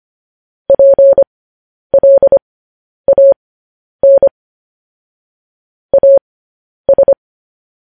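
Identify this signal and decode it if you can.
Morse code, heard as a keyed tone, about 25 wpm, spaced as for 10 wpm, 550 Hz. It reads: PLAN AS
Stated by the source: Morse code news headlines practice transmission.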